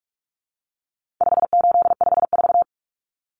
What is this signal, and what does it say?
Morse code, heard as a keyed tone, about 44 wpm, 700 Hz. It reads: 5854
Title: Common Cut Numbers - 4-Digit 45wpm